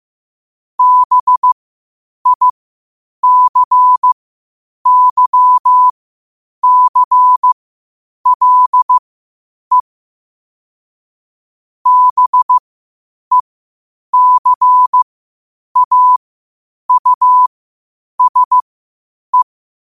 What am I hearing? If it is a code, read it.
Morse code, heard as a keyed tone, about 15 wpm, spaced as for 8 wpm, 1000 Hz. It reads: BICYCLE BECAUSE
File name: 8wpm